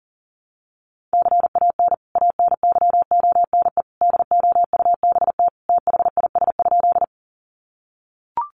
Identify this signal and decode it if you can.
Morse code, heard as a keyed tone, about 40 wpm, 700 Hz. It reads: CAN ANYONE DOUBT THIS?
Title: Sentences from Top 1500 Words 40wpm